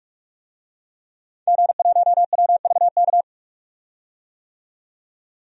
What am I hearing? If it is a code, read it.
Morse code, heard as a keyed tone, about 45 wpm, 700 Hz. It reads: G1WVK
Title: DXCC-100 Call Signs - 5 Characters - 45wpm